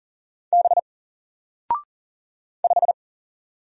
Morse code, heard as a keyed tone, about 40 wpm, 700 Hz. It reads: B 5
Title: Single Letter-Number 40wpm